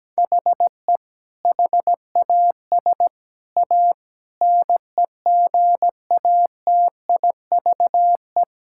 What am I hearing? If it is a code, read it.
Morse code, heard as a keyed tone, about 17 wpm, 700 Hz. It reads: HE HAS A NEGATIVE
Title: Sentences from Top 1000 Words - Review Part 6 - 17wpm